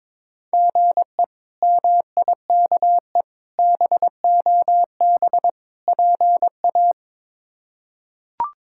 Morse code, heard as a keyed tone, about 22 wpm, 700 Hz. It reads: GE MIKE BOB PA
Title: K1USN SST QSO Practice - 22wpm